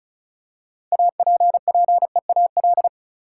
Morse code, heard as a keyed tone, about 34 wpm, 700 Hz. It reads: APPEAL
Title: Six Letter Words - 35wpm